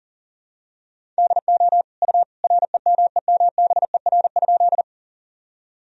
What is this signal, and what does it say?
Morse code, heard as a keyed tone, about 40 wpm, 700 Hz. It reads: DO U REMEMBER?
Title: Sentences from Top 400 Words and Abbreviations 40wpm